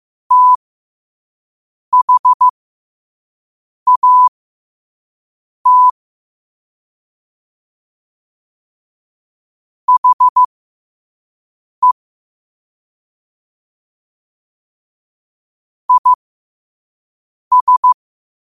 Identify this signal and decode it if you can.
Morse code, heard as a keyed tone, about 15 wpm, spaced as for 5 wpm, 1000 Hz. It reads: THAT HE IS
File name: qotdfiles / 5wpm